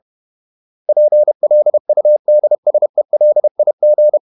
Morse code, heard as a keyed tone, about 31 wpm, 600 Hz. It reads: PLUDSELIG
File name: Ferie QRQ 31wpm